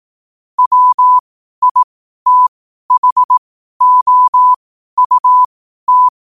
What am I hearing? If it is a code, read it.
Morse code, heard as a keyed tone, about 18 wpm, spaced as for 12 wpm, 1000 Hz. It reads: WITHOUT
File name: qotdfiles / 12wpm